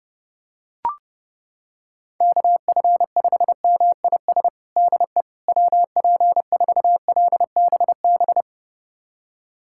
Morse code, heard as a keyed tone, about 32 wpm, 700 Hz. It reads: KF5MIS DE WP4LBB
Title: USA Callsign DE Callsign - 6 Characters - 30wpm